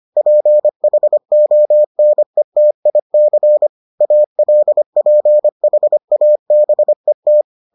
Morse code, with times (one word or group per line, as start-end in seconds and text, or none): 0.2-3.7 PHONETIC
4.0-7.4 ALPHABET